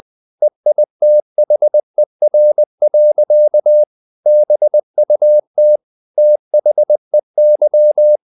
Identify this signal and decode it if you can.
Morse code, heard as a keyed tone, about 20 wpm, 600 Hz. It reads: EITHER. BUT THEY
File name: the old man and the sea part 2 20wpm